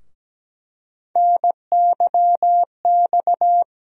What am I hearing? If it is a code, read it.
Morse code, heard as a keyed tone, about 17 wpm, 700 Hz. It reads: NYX
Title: USA Call Sign Postfixes - 3 Letters 17wpm